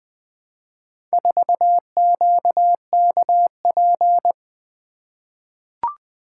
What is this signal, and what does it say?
Morse code, heard as a keyed tone, about 20 wpm, 700 Hz. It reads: 4QKP
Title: Four Letter-Number Combinations - 20wpm